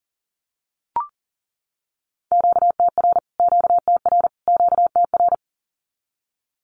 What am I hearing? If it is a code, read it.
Morse code, heard as a keyed tone, about 40 wpm, 700 Hz. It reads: QTR QTR QTR
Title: Q-Codes Repeated 3x 40wpm